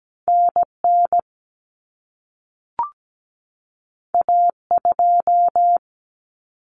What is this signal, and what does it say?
Morse code, heard as a keyed tone, about 17 wpm, 700 Hz. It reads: NN A2
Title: Common Cut Numbers - 2-Digit 17wpm